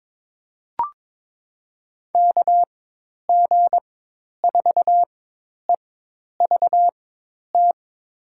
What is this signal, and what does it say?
Morse code, heard as a keyed tone, about 22 wpm, spaced as for 10 wpm, 700 Hz. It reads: KG4EVT
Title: USA Call Signs - 6 Characters - 4x Character Spacing - 22wpm